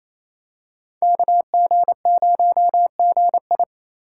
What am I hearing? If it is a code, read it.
Morse code, heard as a keyed tone, about 28 wpm, 700 Hz. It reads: KG0GI
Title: USA Call Signs - 5 Characters - 6x Character Spacing - 28wpm